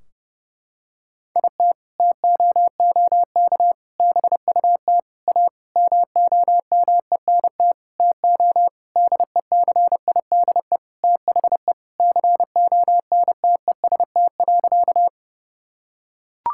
Morse code, heard as a keyed tone, about 32 wpm, 700 Hz. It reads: IT TOOK BUT A MOMENT TO DECIDE THE CONTEST.
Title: Sentences from Top 1600 Words 30wpm